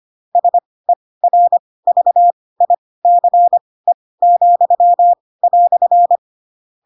Morse code, heard as a keyed tone, about 25 wpm, spaced as for 18 wpm, 700 Hz. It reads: SERVICE,"
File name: Pinocchio / Pinocchio ch2 12 25 ik1hge